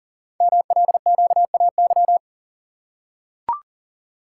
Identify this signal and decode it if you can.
Morse code, heard as a keyed tone, about 40 wpm, 700 Hz. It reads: MLQAY
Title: Code Groups - 5 Letters - 40wpm